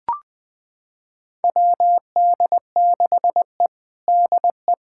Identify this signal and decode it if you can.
Morse code, heard as a keyed tone, about 20 wpm, 700 Hz. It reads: WD6E DE
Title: USA Callsign DE Callsign - 4 Characters - 20wpm